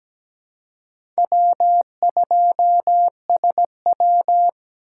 Morse code, heard as a keyed tone, about 17 wpm, 700 Hz. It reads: W2SW
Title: USA Call Signs - 4 Characters - 6x Character Spacing - 17wpm